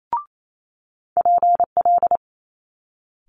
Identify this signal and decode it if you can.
Morse code, heard as a keyed tone, about 28 wpm, 700 Hz. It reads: PL
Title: Multi-letter Phonemes Quick Fire 28wpm